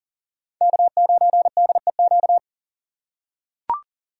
Morse code, heard as a keyed tone, about 38 wpm, 700 Hz. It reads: K9DEQ
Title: USA Call Signs - 5 Characters - 2x Character Spacing - 40wpm